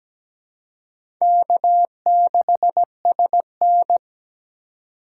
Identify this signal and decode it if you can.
Morse code, heard as a keyed tone, about 17 wpm, 700 Hz. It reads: K6SN
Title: USA Call Signs - 4 Characters - 4x Character Spacing - 17wpm